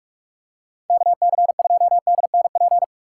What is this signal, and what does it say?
Morse code, heard as a keyed tone, about 46 wpm, 700 Hz. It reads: KC2DNP
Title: USA Call Signs - 6 Characters - 4x Character Spacing - 45wpm